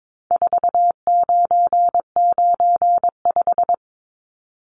Morse code, heard as a keyed tone, about 22 wpm, 700 Hz. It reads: 4995
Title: Numbers - 4-Digit 22wpm